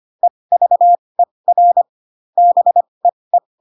Morse code, heard as a keyed tone, about 25 wpm, spaced as for 20 wpm, 700 Hz. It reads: EVER BEE